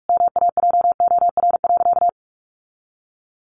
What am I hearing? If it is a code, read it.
Morse code, heard as a keyed tone, about 44 wpm, 700 Hz. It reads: MAJOR.
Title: Sentences from Top 600 Words 45wpm